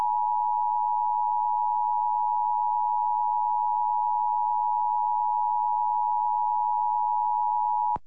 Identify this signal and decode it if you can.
Emergency Alert System (SAME) tones (the 853 and 960 Hz attention tone). It is heard from 0.9 to 7.0 s.